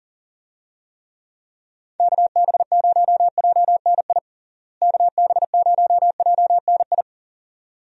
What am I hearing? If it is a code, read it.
Morse code, heard as a keyed tone, about 37 wpm, 700 Hz. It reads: KB0JNI KB0JNI
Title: USA Call Signs - 6 Characters - Repeated 2x - 40wpm